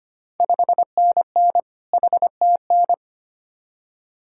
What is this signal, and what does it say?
Morse code, heard as a keyed tone, about 25 wpm, 700 Hz. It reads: 5NN HTN
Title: Ontario QSO Party Exchange 25wpm